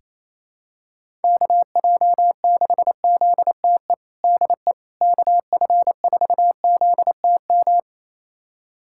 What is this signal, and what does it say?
Morse code, heard as a keyed tone, about 28 wpm, 700 Hz. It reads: KJ6ZTE DE KF4ZTM
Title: USA Callsign DE Callsign - 6 Characters - 28wpm